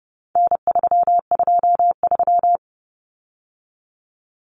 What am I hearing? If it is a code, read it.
Morse code, heard as a keyed tone, about 30 wpm, 700 Hz. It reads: N323
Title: Common Cut Numbers - 4-Digit 30wpm